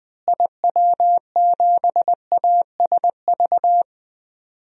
Morse code, heard as a keyed tone, about 21 wpm, 700 Hz. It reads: IW7ASV
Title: DXCC-100 Call Signs - 6 Characters - Repeated 2x - 20wpm